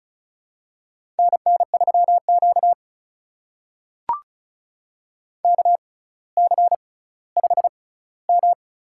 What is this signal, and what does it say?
Morse code, heard as a keyed tone, about 36 wpm, spaced as for 18 wpm, 700 Hz. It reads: NN3Q KC5M
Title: USA Call Signs - 4 Characters - 6x Character Spacing - 35wpm